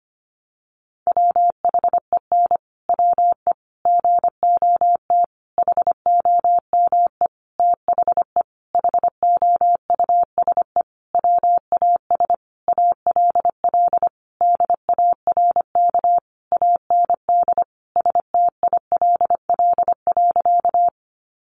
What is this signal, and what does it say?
Morse code, heard as a keyed tone, about 25 wpm, 700 Hz. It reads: WHEN WE GOT HOME THE HOUSE WAS ALL DARK AND STILL.
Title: Sentences from Top 500 Words 25wpm